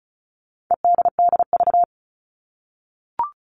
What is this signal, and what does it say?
Morse code, heard as a keyed tone, about 34 wpm, 700 Hz. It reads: EDDV
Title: All Cut Numbers - 4-Digit 35wpm